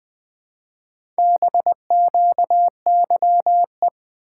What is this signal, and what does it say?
Morse code, heard as a keyed tone, about 20 wpm, 700 Hz. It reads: BQYE
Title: Four Letter-Number Combinations - 20wpm